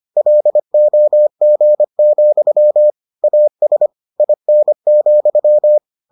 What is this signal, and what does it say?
Morse code, heard as a keyed tone, about 25 wpm, 600 Hz. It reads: LOG, AS IN,